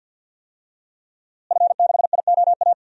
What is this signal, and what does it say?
Morse code, heard as a keyed tone, about 53 wpm, 700 Hz. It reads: F6IOA
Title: DXCC-100 Call Signs - 5 Characters - Repeated 3x - 50wpm